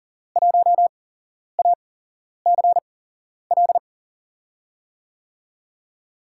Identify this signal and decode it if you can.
Morse code, heard as a keyed tone, about 40 wpm, spaced as for 11 wpm, 700 Hz. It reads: 1ACL
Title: USA Call Signs - 5 Characters - 8x Character Spacing - 40wpm